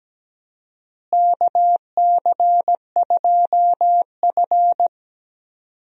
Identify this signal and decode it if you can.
Morse code, heard as a keyed tone, about 17 wpm, 700 Hz. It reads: KC2F
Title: USA Call Signs - 4 Characters - Repeated 2x - 17wpm